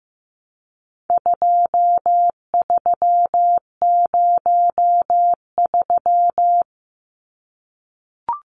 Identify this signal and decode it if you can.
Morse code, heard as a keyed tone, about 15 wpm, 700 Hz. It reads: 2303
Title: Numbers - 4-Digit 15wpm